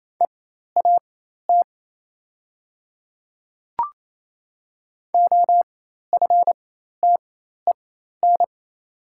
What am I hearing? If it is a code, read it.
Morse code, heard as a keyed tone, about 28 wpm, spaced as for 13 wpm, 690 Hz. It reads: EAT OFTEN
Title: 4x Character Spacing - Top 500 Words - 28wpm